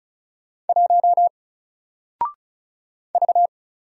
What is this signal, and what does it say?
Morse code, heard as a keyed tone, about 34 wpm, 700 Hz. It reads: 1 V
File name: Single-Letter-Number-35wpm